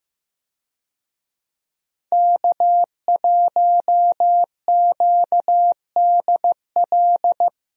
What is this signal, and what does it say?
Morse code, heard as a keyed tone, about 15 wpm, 700 Hz. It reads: K1QDL
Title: USA Call Signs - 5 Characters - Repeated 3x - 15wpm